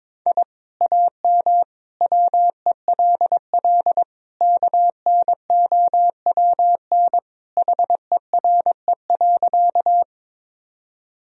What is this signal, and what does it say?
Morse code, heard as a keyed tone, about 21 wpm, 700 Hz. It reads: I AM WELL KNOWN HERE.
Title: Sentences from Top 1000 Words - Review Part 3 - 22wpm